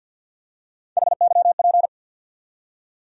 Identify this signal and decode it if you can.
Morse code, heard as a keyed tone, about 52 wpm, 700 Hz. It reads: HYP
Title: USA Call Sign Postfixes - 3 Letters 50wpm